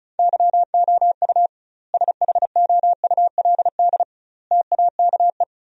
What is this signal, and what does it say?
Morse code, heard as a keyed tone, about 36 wpm, 700 Hz. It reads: YOU SHOULD TAKE
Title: 2x Word-Spacing - Sentences from Top 300 Words - 35wpm